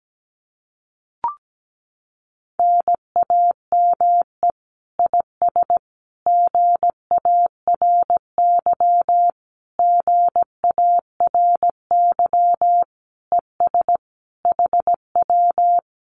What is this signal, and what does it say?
Morse code, heard as a keyed tone, about 17 wpm, 700 Hz. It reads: NAME IS GARY GARY ES HW
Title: Common QSO Phrases Encore 17wpm